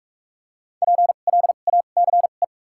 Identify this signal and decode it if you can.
Morse code, heard as a keyed tone, about 47 wpm, spaced as for 33 wpm, 700 Hz. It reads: PLACE
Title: 2x Character Spacing - Top 200 Words - 45wpm